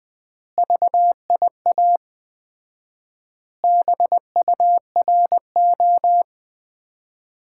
Morse code, heard as a keyed tone, about 20 wpm, 700 Hz. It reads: VIA BURO
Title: Common QSO Phrases - 4x Word-Spacing - 20wpm